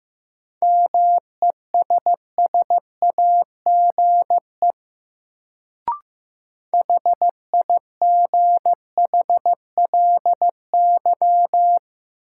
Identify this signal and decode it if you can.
Morse code, heard as a keyed tone, about 15 wpm, 700 Hz. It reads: MESSAGE HIGHLY
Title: Sentences from Top 1100 Words - New Word Drill - 15wpm